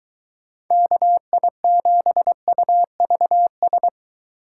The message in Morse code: KI7UVS

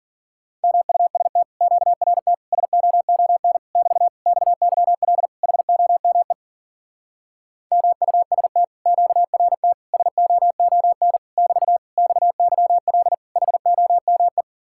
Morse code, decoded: MUST QRT SOON <BT> XYL HOME MUST QRT SOON <BT> XYL HOME

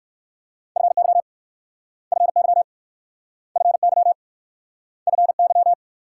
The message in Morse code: FY FY FY FY

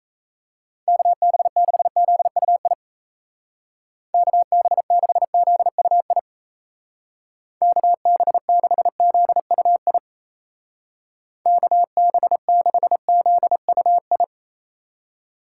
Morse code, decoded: KB6ZUI KB6ZUI KB6ZUI KB6ZUI

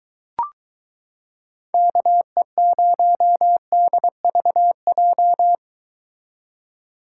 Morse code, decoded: KE0DVJ

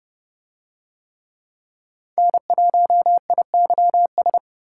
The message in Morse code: N1IYS

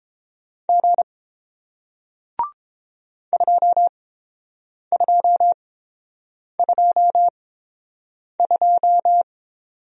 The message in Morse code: G 2222

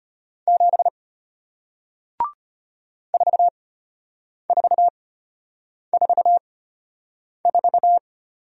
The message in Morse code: 7 4444

